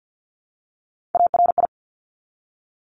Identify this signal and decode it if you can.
Morse code, heard as a keyed tone, about 50 wpm, 700 Hz. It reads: ARI
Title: Top Three Letter Combinations in Words 50wpm